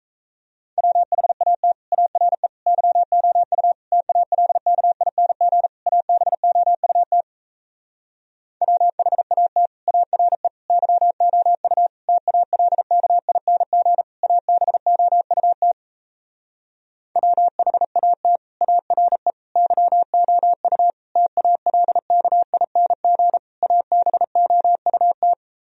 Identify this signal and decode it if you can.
Morse code, heard as a keyed tone, about 37 wpm, 700 Hz. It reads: WHAT ARE YOU TALKING ABOUT WHAT ARE YOU TALKING ABOUT WHAT ARE YOU TALKING ABOUT